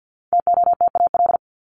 Morse code, heard as a keyed tone, about 50 wpm, 700 Hz. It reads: TOTAL
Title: LY Suffix - Top 100 Words with Base Words 50wpm